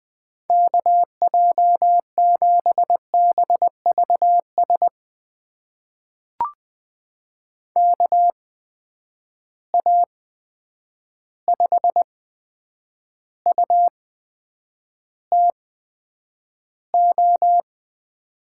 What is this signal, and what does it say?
Morse code, heard as a keyed tone, about 20 wpm, spaced as for 9 wpm, 700 Hz. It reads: KJ7BVS KA5UTO
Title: USA Call Signs - 6 Characters - 8x Character Spacing - 20wpm